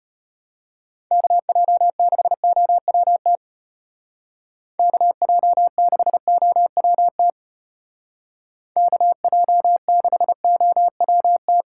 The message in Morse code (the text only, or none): KJ6OWT KJ6OWT KJ6OWT